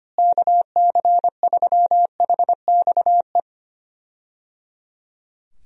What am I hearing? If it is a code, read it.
Morse code, heard as a keyed tone, about 25 wpm, 700 Hz. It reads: KC3HXE
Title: USA Call Signs - 6 Characters - Repeated 3x - 25wpm